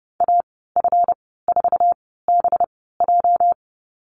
Morse code, heard as a keyed tone, about 30 wpm, spaced as for 17 wpm, 700 Hz. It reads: AF4BJ
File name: Lesson 37g - Letter x - Call Signs 30wpm